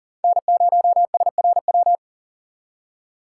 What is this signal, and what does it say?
Morse code, heard as a keyed tone, about 40 wpm, 700 Hz. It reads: N0SRW
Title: USA Call Signs - 5 Characters - 3x Character Spacing - 40wpm